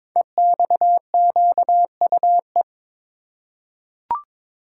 Morse code, heard as a keyed tone, about 22 wpm, 700 Hz. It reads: EXQUE
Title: Code Groups - 5 Letter:Numbers - 22wpm